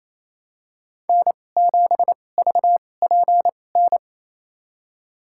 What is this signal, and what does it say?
Morse code, heard as a keyed tone, about 28 wpm, spaced as for 20 wpm, 700 Hz. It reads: N7VPN
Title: USA Call Signs - 5 Characters - 2x Character Spacing - 28wpm